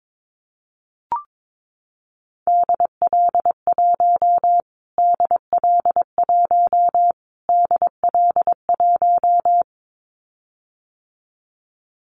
Morse code, heard as a keyed tone, about 22 wpm, 700 Hz. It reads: DL1 DL1 DL1
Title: Callsign Prefixes for Top 25 DXCC - Repeated 3x - 22wpm